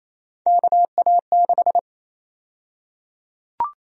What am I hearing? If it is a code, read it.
Morse code, heard as a keyed tone, about 28 wpm, 700 Hz. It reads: KA6
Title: USA Call Sign Prefixes - 28wpm